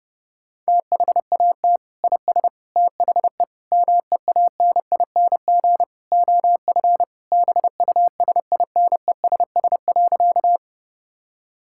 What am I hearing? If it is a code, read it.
Morse code, heard as a keyed tone, about 30 wpm, 700 Hz. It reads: THAT IS THE MEANING OF BUSINESS.
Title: Sentences from Top 1000 Words - Review Part 6 - 30wpm